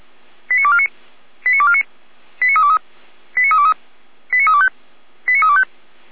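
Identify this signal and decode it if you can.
Five-tone selective call (ZVEI), heard as quick runs of five tones, tones of about 70 ms. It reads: 89249 89249 89323 89323 89325 89325
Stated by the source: ZVEI five-tone call received by SDR, archived by fire brigade Feuerwehr Orlamünde.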